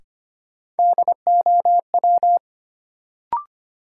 Morse code, heard as a keyed tone, about 25 wpm, 700 Hz. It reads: DOW
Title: Sets of 5 Words - New Word Drill - 25wpm